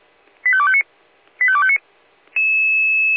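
Five-tone selective call (ZVEI), heard as quick runs of five tones, tones of about 70 ms. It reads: 85259 85259
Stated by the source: ZVEI five-tone call received by SDR.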